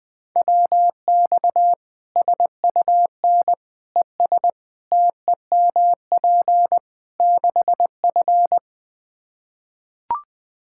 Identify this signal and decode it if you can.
Morse code, heard as a keyed tone, about 21 wpm, 700 Hz. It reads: WX SUN ES TEMP 6F
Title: Practice QSO - Exchange 2 - Info - 20wpm